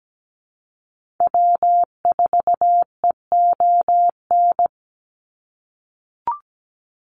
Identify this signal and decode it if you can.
Morse code, heard as a keyed tone, about 17 wpm, 700 Hz. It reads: W4EON
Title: NA SOTA Activator Callsigns 17wpm